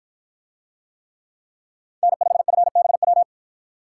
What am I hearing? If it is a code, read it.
Morse code, heard as a keyed tone, about 53 wpm, 700 Hz. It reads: N5FBW